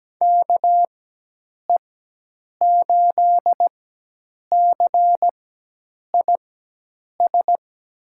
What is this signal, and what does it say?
Morse code, heard as a keyed tone, about 17 wpm, spaced as for 8 wpm, 700 Hz. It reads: KE8CIS
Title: USA Call Signs - 6 Characters - 4x Character Spacing - 17wpm